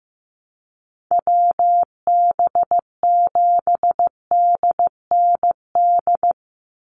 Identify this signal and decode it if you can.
Morse code, heard as a keyed tone, about 15 wpm, 700 Hz. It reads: WB7DND
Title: Call Signs Repeated 2x - 15wpm